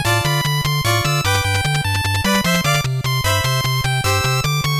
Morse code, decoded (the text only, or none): none